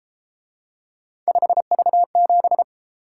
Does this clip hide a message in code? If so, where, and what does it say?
Yes: Morse code from 1.3 to 2.6 s, heard as a keyed tone, about 31 wpm, 700 Hz. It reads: 5V7